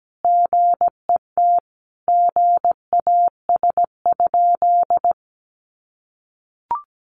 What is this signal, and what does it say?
Morse code, heard as a keyed tone, about 17 wpm, 700 Hz. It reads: GET GAS?
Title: Sentences from Top 1100 Words 17wpm